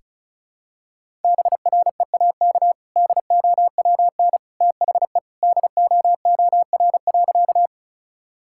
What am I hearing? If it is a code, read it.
Morse code, heard as a keyed tone, about 34 wpm, 700 Hz. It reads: BREAK DOWN THE DOOR.
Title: Sentences from Top 1000 Words - Review Part 6 - 35wpm